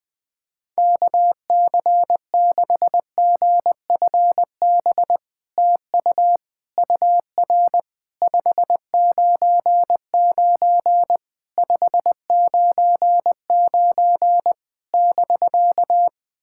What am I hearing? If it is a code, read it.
Morse code, heard as a keyed tone, about 20 wpm, 700 Hz. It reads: KC6GFB TU UR 599 599 <BK>
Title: SOTA QSO Practice 20wpm